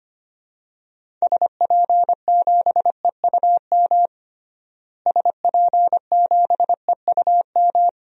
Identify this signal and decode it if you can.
Morse code, heard as a keyed tone, about 25 wpm, 700 Hz. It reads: SP7EUM SP7EUM
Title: DXCC-100 Call Signs - 6 Characters - Repeated 2x - 25wpm